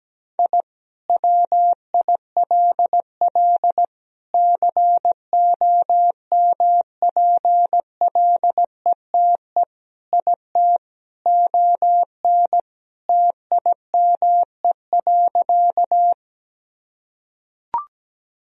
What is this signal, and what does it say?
Morse code, heard as a keyed tone, about 17 wpm, 700 Hz. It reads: I WILL COMPLETE IT ON TIME.
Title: Sentences from Top 1000 Words - Review Part 2 - 17wpm